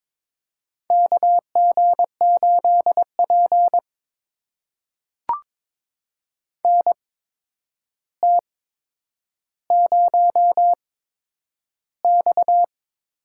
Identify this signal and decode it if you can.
Morse code, heard as a keyed tone, about 22 wpm, spaced as for 9 wpm, 700 Hz. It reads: KG8P NT0X
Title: USA Call Signs - 4 Characters - 8x Character Spacing - 22wpm